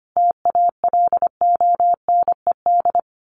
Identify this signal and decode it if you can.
Morse code, heard as a keyed tone, about 25 wpm, 700 Hz. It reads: TALONED
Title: A Sound Of Thunder 25wpm